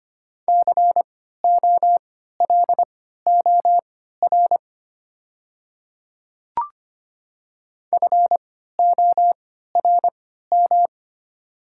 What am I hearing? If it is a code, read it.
Morse code, heard as a keyed tone, about 25 wpm, spaced as for 14 wpm, 700 Hz. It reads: COLOR FORM
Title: 3x Character Spacing - Top 300 Words - 25wpm